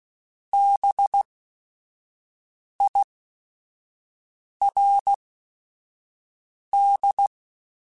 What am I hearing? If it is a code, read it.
Morse code, heard as a keyed tone, about 16 wpm, spaced as for 5 wpm, 800 Hz. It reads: BIRD
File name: General / General-mp3s / Houses @5